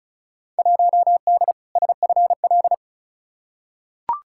Morse code, heard as a keyed tone, about 36 wpm, 700 Hz. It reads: 1D SFL
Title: Field Day Exchanges - 35wpm